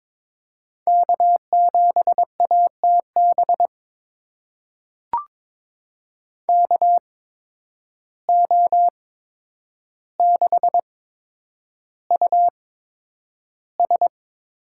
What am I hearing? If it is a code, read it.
Morse code, heard as a keyed tone, about 21 wpm, spaced as for 9 wpm, 700 Hz. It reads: K7ATB KO6US
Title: USA Call Signs - 5 Characters - 8x Character Spacing - 22wpm